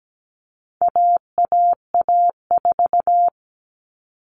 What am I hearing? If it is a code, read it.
Morse code, heard as a keyed tone, about 17 wpm, 700 Hz. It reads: AAA4
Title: All Cut Numbers - 4-Digit 17wpm